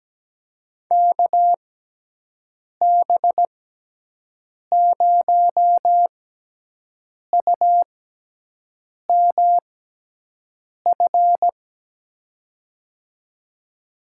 Morse code, heard as a keyed tone, about 17 wpm, 700 Hz. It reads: K B 0 U M F
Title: USA Call Signs - 6 Characters - 6x Character Spacing - 17wpm